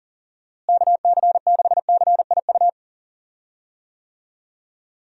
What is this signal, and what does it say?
Morse code, heard as a keyed tone, about 37 wpm, 700 Hz. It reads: KC6CIU
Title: USA Call Signs - 6 Characters - Repeated 3x - 40wpm